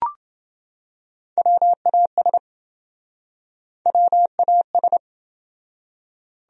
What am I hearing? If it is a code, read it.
Morse code, heard as a keyed tone, about 27 wpm, 700 Hz. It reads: WAS WAS